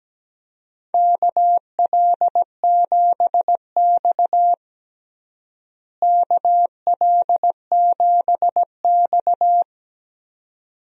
Morse code, heard as a keyed tone, about 17 wpm, 700 Hz. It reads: KL7X KL7X
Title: DXCC-100 Call Signs - 4 Characters - Repeated 2x - 17wpm